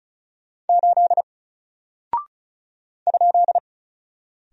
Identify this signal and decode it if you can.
Morse code, heard as a keyed tone, about 36 wpm, 700 Hz. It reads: 8 ?
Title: Single Character (Letter:Number:Punctuation:Prosign) Alt 35wpm